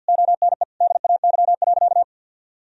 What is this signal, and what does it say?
Morse code, heard as a keyed tone, about 50 wpm, 700 Hz. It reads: ONE DAY.
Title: Sentences from Top 1000 Words - Review Part 3 - 50wpm